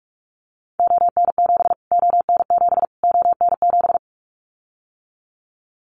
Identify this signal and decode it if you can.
Morse code, heard as a keyed tone, about 44 wpm, 700 Hz. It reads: ON7 ON7 ON7
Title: Callsign Prefixes for Top 25 DXCC - Repeated 3x - 45wpm